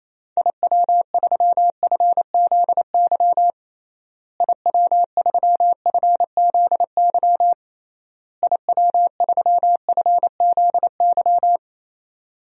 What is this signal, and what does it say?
Morse code, heard as a keyed tone, about 28 wpm, 700 Hz. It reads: IW3FZY IW3FZY IW3FZY
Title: DXCC-100 Call Signs - 6 Characters - Repeated 3x - 28wpm